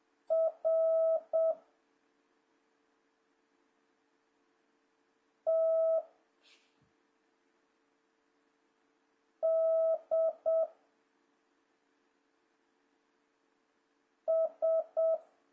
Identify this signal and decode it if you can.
Morse code, heard as a keyed tone, about 7 wpm, 650 Hz. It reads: R T D S
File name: jh8cbh / 2026cw